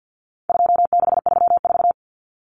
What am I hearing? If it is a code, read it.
Morse code, heard as a keyed tone, about 50 wpm, 700 Hz. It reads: 2634
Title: Numbers - 4-Digit 50wpm